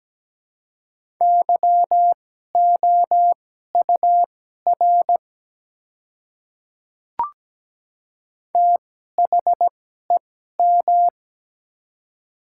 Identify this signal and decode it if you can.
Morse code, heard as a keyed tone, about 17 wpm, spaced as for 12 wpm, 700 Hz. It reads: YOUR THEM
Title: 2x Character Spacing - Top 100 Words - 17wpm